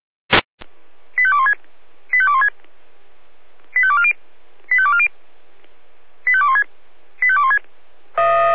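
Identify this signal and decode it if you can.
Five-tone selective call (ZVEI), heard as quick runs of five tones, tones of about 80 ms. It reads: 85216 85216 85240 85240 85216 85216
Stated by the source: ZVEI five-tone selective call received on an SDR.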